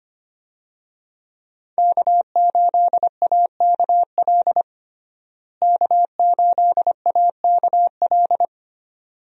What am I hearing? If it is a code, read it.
Morse code, heard as a keyed tone, about 25 wpm, 700 Hz. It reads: K8AKL K8AKL
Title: USA Call Signs - 5 Characters - Repeated 2x - 25wpm